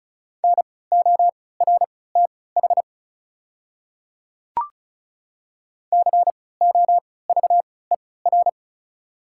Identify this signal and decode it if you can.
Morse code, heard as a keyed tone, about 34 wpm, spaced as for 20 wpm, 700 Hz. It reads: NORTH COVER
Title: 3x Character Spacing - Top 300 Words - 35wpm